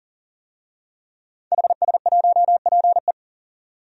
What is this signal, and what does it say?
Morse code, heard as a keyed tone, about 40 wpm, 700 Hz. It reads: HS1PE
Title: DXCC-100 Call Signs - 5 Characters - 40wpm